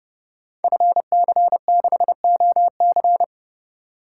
Morse code, heard as a keyed tone, about 31 wpm, 700 Hz. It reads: FC6OC
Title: Code Groups - 5 Letter:Numbers - 30wpm